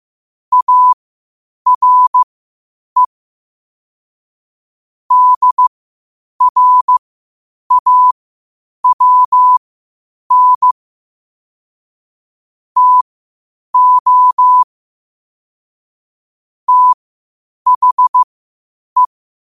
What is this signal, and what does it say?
Morse code, heard as a keyed tone, about 15 wpm, spaced as for 8 wpm, 1000 Hz. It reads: ARE DRAWN TO THE
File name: qotdfiles / 8wpm